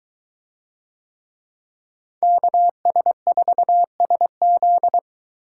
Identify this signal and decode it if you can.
Morse code, heard as a keyed tone, about 21 wpm, 700 Hz. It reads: KS4SZ